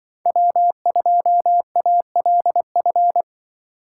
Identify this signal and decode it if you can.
Morse code, heard as a keyed tone, about 24 wpm, 700 Hz. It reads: W2ALF
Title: Speed Racing - USA 1x3 Call Signs 20wpm